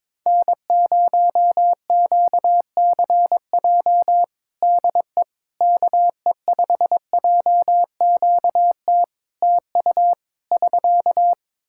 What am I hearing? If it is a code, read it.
Morse code, heard as a keyed tone, about 22 wpm, 700 Hz. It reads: N0QCJ DE KE5JQT TU <SK>
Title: Practice QSO - Exchange 4 - Ending - 22wpm